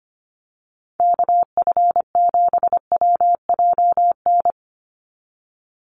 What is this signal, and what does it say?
Morse code, heard as a keyed tone, about 25 wpm, 700 Hz. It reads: KF7WJN
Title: USA Call Signs 25wpm